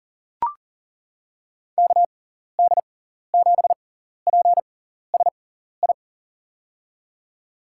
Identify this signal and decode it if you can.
Morse code, heard as a keyed tone, about 39 wpm, spaced as for 14 wpm, 700 Hz. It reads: KD7PSI
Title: USA Call Signs - 6 Characters - 6x Character Spacing - 40wpm